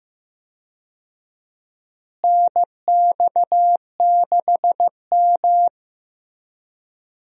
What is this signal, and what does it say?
Morse code, heard as a keyed tone, about 15 wpm, 700 Hz. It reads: NX6M
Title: USA Call Signs - 4 Characters - Repeated 2x - 15wpm